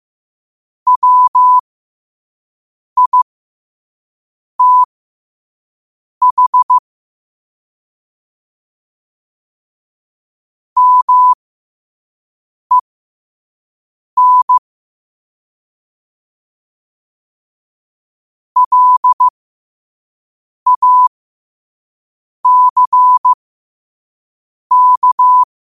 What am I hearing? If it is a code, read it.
Morse code, heard as a keyed tone, about 15 wpm, spaced as for 5 wpm, 1000 Hz. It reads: WITH MEN LACK
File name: qotdfiles / 5wpm